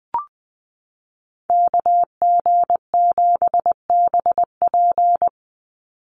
Morse code, heard as a keyed tone, about 20 wpm, 700 Hz. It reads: KG7BP
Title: USA Call Signs 20wpm